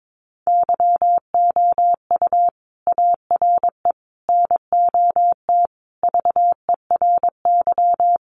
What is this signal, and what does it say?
Morse code, heard as a keyed tone, about 22 wpm, 700 Hz. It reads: YOU ARE NOT VERY
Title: Sentences from Top 1200 Words 22wpm